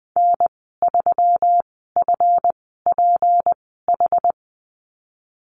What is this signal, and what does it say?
Morse code, heard as a keyed tone, about 20 wpm, spaced as for 15 wpm, 700 Hz. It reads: N3FPH
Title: 200 Lesson 32g - Letter k - Call Signs 20wpm